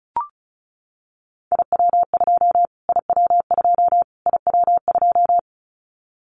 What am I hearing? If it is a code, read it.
Morse code, heard as a keyed tone, about 34 wpm, 700 Hz. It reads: IW2 IW2 IW2
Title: Callsign Prefixes for Top 25 DXCC - Repeated 3x - 35wpm